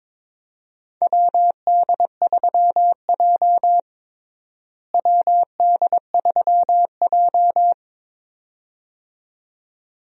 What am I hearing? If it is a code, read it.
Morse code, heard as a keyed tone, about 23 wpm, 700 Hz. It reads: WD3J WD3J
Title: USA Call Signs - 4 Characters - Repeated 2x - 22wpm